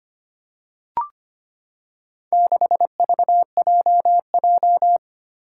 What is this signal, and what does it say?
Morse code, heard as a keyed tone, about 25 wpm, 700 Hz. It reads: 6VJJ
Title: Four Challenging Letter-Number Combinations 25wpm